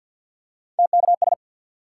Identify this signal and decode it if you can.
Morse code, heard as a keyed tone, about 54 wpm, 700 Hz. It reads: TKS
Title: Top 90 QSO Elements 50wpm